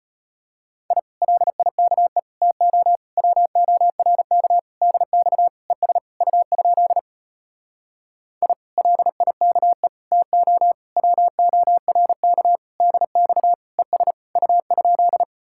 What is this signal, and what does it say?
Morse code, heard as a keyed tone, about 33 wpm, 700 Hz. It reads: I LIKE TO WORK DX ES U? I LIKE TO WORK DX ES U?